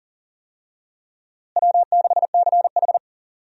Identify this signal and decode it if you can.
Morse code, heard as a keyed tone, about 40 wpm, 700 Hz. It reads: W6CH
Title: USA Call Signs - 4 Characters - Repeated 3x - 40wpm